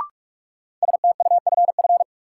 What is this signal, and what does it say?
Morse code, heard as a keyed tone, about 43 wpm, 700 Hz. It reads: STUFF
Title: Sentences from Top 1400 Words - New Word Drill - 45wpm